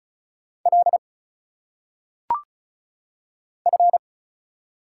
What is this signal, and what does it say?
Morse code, heard as a keyed tone, about 34 wpm, 700 Hz. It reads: L F
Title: Single Letters - Phonetic Pronunciation 35wpm